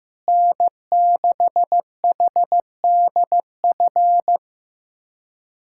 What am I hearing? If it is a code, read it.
Morse code, heard as a keyed tone, about 15 wpm, 700 Hz. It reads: N6HDF
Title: USA Callsign DE Callsign 15wpm